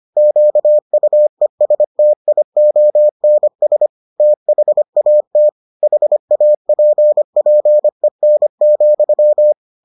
Morse code, heard as a keyed tone, about 25 wpm, 600 Hz. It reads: QUESTIONS THAT HAPPEN,